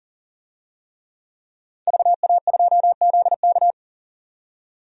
Morse code, heard as a keyed tone, about 40 wpm, 700 Hz. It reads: VA2ZK
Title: DXCC-100 Call Signs - 5 Characters - 40wpm